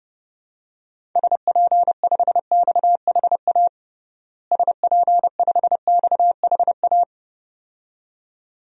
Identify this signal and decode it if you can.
Morse code, heard as a keyed tone, about 32 wpm, 700 Hz. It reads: SP5XHA SP5XHA
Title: DXCC-100 Call Signs - 6 Characters - Repeated 2x - 30wpm